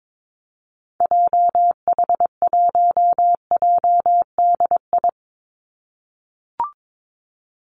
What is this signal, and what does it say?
Morse code, heard as a keyed tone, about 22 wpm, 700 Hz. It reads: JH1JDI
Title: Call Signs - 22wpm